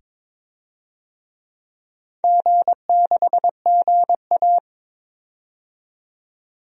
Morse code, heard as a keyed tone, about 22 wpm, 700 Hz. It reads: G6GA
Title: DXCC-100 Call Signs - 4 Characters - 22wpm